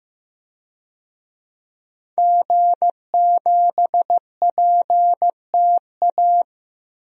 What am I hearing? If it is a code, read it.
Morse code, heard as a keyed tone, about 16 wpm, 700 Hz. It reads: G7PTA
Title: DXCC-100 Call Signs - 5 Characters - 15wpm